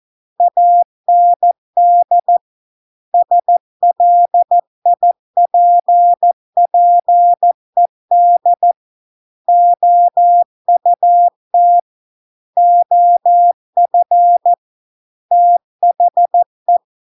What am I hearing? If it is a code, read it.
Morse code, heard as a keyed tone, about 14 wpm, 700 Hz. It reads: AND SLIPPED OUT OF THE